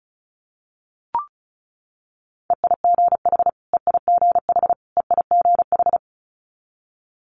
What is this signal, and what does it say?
Morse code, heard as a keyed tone, about 36 wpm, 700 Hz. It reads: EIGH EIGH EIGH
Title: Multi-letter Phonemes Repeated 3x 35wpm